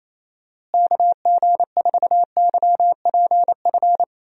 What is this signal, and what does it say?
Morse code, heard as a keyed tone, about 28 wpm, 700 Hz. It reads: KG4YPF
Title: USA Call Signs - 6 Characters - Repeated 3x - 28wpm